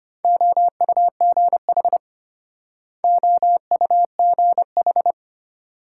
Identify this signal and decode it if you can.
Morse code, heard as a keyed tone, about 33 wpm, 700 Hz. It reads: OUGH OUGH